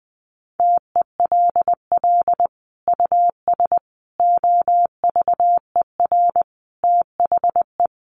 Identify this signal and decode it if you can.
Morse code, heard as a keyed tone, about 20 wpm, 700 Hz. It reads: TELL US OVER THE
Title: Sentences from Top 1100 Words 20wpm